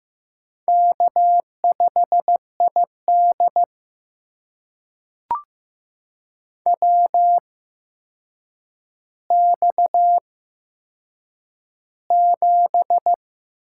Morse code, heard as a keyed tone, about 15 wpm, spaced as for 7 wpm, 700 Hz. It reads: K5ID WX7
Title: USA Call Signs - 4 Characters - 8x Character Spacing - 15wpm